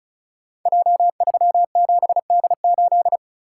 Morse code, heard as a keyed tone, about 36 wpm, 700 Hz. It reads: J37D8
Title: Code Groups - 5 Letter:Numbers - 35wpm